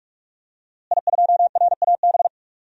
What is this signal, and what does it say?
Morse code, heard as a keyed tone, about 43 wpm, 700 Hz. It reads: I2RAB
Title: DXCC-100 Call Signs - 5 Characters - Repeated 3x - 45wpm